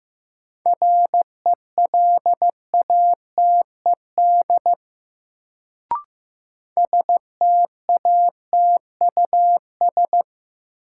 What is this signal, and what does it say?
Morse code, heard as a keyed tone, about 15 wpm, 700 Hz. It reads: RELATED STATUS
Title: Sentences from Top 1400 Words - New Word Drill Alt - 15wpm